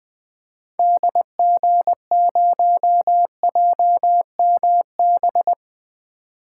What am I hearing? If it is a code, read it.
Morse code, heard as a keyed tone, about 20 wpm, 700 Hz. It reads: DG0JMB
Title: 2023 SOTA Activator Call Signs - 20wpm